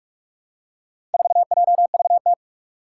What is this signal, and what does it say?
Morse code, heard as a keyed tone, about 46 wpm, 700 Hz. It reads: 4JVT
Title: Four Challenging Letter-Number Combinations 45wpm